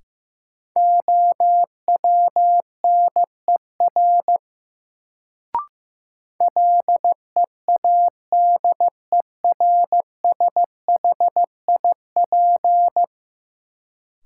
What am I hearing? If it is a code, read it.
Morse code, heard as a keyed tone, about 15 wpm, 700 Hz. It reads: OWNER LEADERSHIP
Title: Sentences from Top 1600 Words - New Word Drill Alt - 15wpm